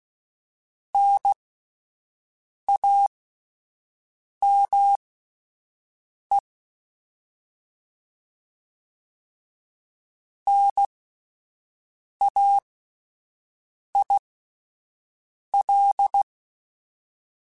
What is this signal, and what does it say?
Morse code, heard as a keyed tone, about 16 wpm, spaced as for 5 wpm, 800 Hz. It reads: NAME NAIL
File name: Alphabetical / Alphabetical mp3s / Ns @5